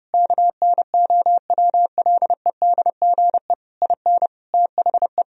Morse code, heard as a keyed tone, about 30 wpm, 700 Hz. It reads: KNOWLEDGE IN THE
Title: Sentences from Top 1000 Words - Review Part 4 - 30wpm